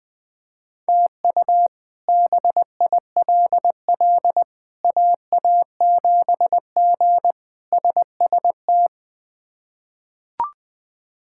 Morse code, heard as a keyed tone, about 20 wpm, 700 Hz. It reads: TU BILL AA7G SST
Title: K1USN SST QSO Practice - 20wpm